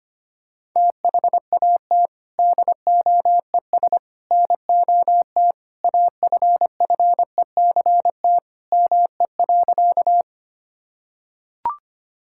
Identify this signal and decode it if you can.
Morse code, heard as a keyed tone, about 25 wpm, 700 Hz. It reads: THAT DOES NOT AFFECT ME.
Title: Sentences from Top 1000 Words - Review Part 6 - 25wpm